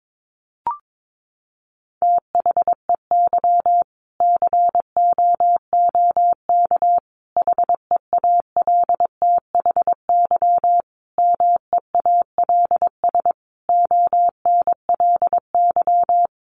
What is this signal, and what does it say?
Morse code, heard as a keyed tone, about 22 wpm, 700 Hz. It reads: THEY COOK HEALTHY MEALS ONLY
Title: Easy Phrases 22wpm